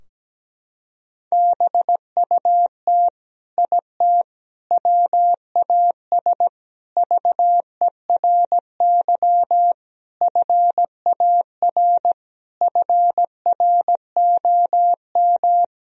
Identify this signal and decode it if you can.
Morse code, heard as a keyed tone, about 17 wpm, 700 Hz. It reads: BUT IT WAS VERY FAR FROM
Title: Sentences from Top 1000 Words - Review Part 2 - 17wpm